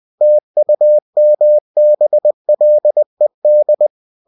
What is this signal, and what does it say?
Morse code, heard as a keyed tone, about 20 wpm, 600 Hz. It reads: TUMBLED
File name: cw-books / winnie-the-pooh / winnie-20wpm-page-0082